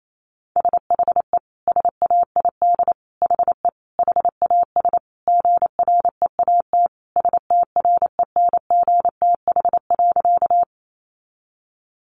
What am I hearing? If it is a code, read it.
Morse code, heard as a keyed tone, about 28 wpm, 700 Hz. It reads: SHE SAID HE HAS GREAT STRENGTH.
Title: Sentences from Top 1300 Words 28wpm